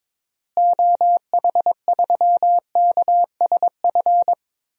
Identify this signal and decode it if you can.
Morse code, heard as a keyed tone, about 22 wpm, 700 Hz. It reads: OH3KSF
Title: DXCC-100 Call Signs - 6 Characters - Repeated 2x - 22wpm